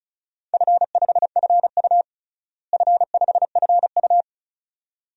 Morse code, heard as a keyed tone, about 36 wpm, 700 Hz. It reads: F5FU F5FU
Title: DXCC-100 Call Signs - 4 Characters - Repeated 2x - 35wpm